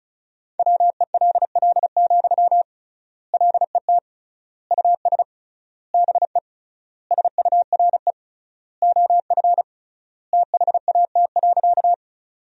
Morse code, heard as a keyed tone, about 36 wpm, 700 Hz. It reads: WELL, LET US BE SURE OF THAT.
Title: 3x Word-Spacing - Sentences from Top 300 Words - 35wpm